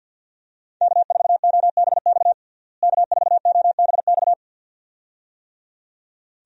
Morse code, heard as a keyed tone, about 51 wpm, 700 Hz. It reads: K4OBX K4OBX
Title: USA Call Signs - 5 Characters - Repeated 2x - 50wpm